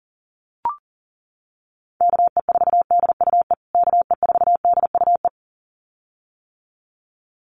Morse code, heard as a keyed tone, about 37 wpm, 700 Hz. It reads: KE4DUE KE4DUE
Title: USA Call Signs Repeated 2x 40wpm